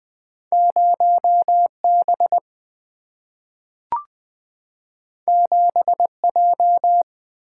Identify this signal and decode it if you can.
Morse code, heard as a keyed tone, about 20 wpm, 700 Hz. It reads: 0B 7J
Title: Single-Letter Single-Number 20wpm